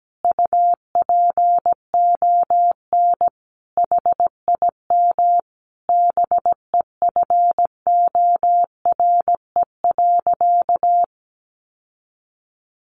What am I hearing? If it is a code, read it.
Morse code, heard as a keyed tone, about 17 wpm, 700 Hz. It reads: UPON HIM BEFORE.
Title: Sentences from Top 1100 Words 17wpm